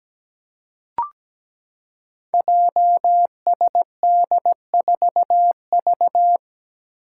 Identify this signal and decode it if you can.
Morse code, heard as a keyed tone, about 17 wpm, 700 Hz. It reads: JSD4V
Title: Five Challenging Letter-Number Combinations 17wpm